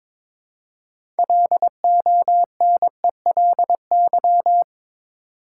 Morse code, heard as a keyed tone, about 22 wpm, 700 Hz. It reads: LONELY
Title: Six Letter Words - 22wpm